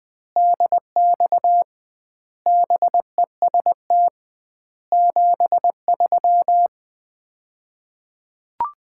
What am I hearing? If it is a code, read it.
Morse code, heard as a keyed tone, about 20 wpm, 700 Hz. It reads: DX BEST 73
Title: Common QSO Phrases - 2x Word-Spacing - 20wpm